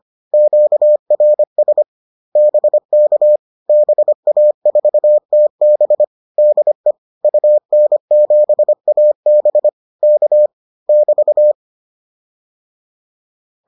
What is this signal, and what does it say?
Morse code, heard as a keyed tone, about 25 wpm, 600 Hz. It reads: QRS BK BA4TB DE UN7AB K <BT>